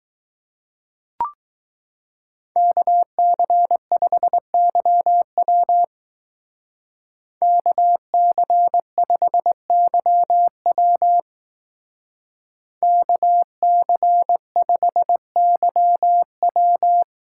Morse code, heard as a keyed tone, about 20 wpm, 700 Hz. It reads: KC5YW KC5YW KC5YW